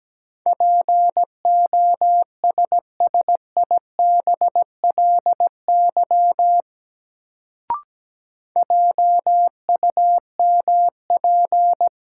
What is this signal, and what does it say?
Morse code, heard as a keyed tone, about 17 wpm, 700 Hz. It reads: POSSIBLY JUMP
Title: Sentences from Top 1300 Words - New Word Drill Alt - 17wpm